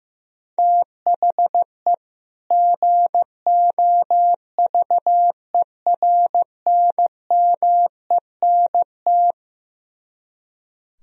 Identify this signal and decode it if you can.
Morse code, heard as a keyed tone, about 15 wpm, 700 Hz. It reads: THE GOVERNMENT
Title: Speed-Racing - Sets of 2 Words 15wpm